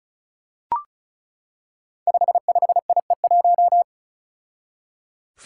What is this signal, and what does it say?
Morse code, heard as a keyed tone, about 36 wpm, 700 Hz. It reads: 55IE1
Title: Five Challenging Letter-Number Combinations 35wpm